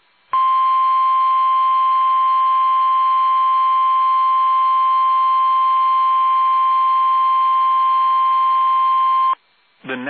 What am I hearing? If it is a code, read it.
Emergency Alert System (SAME) tones (the 1050 Hz attention tone). It is heard from 0.3 to 9.4 s.